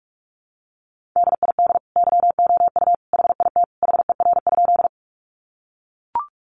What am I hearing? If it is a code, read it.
Morse code, heard as a keyed tone, about 44 wpm, 700 Hz. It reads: DID YOU HIT HER?
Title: Sentences from Top 1200 Words 45wpm